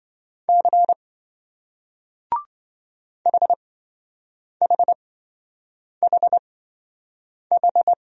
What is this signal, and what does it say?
Morse code, heard as a keyed tone, about 28 wpm, spaced as for 7 wpm, 700 Hz. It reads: C HHHH